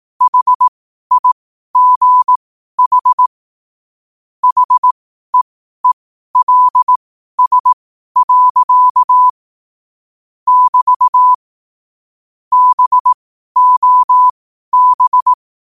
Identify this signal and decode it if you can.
Morse code, heard as a keyed tone, about 18 wpm, spaced as for 12 wpm, 1000 Hz. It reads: HIGH HEELS. <BT> BOB
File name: qotdfiles / 12wpm